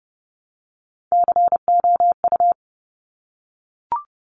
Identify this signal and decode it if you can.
Morse code, heard as a keyed tone, about 30 wpm, 700 Hz. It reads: COU
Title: Top Three Letter Combinations in Words 30wpm